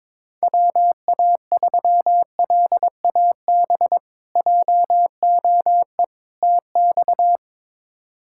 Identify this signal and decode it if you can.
Morse code, heard as a keyed tone, about 22 wpm, 700 Hz. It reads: WA3LAB JOE TX
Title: K1USN SST QSO Practice - 22wpm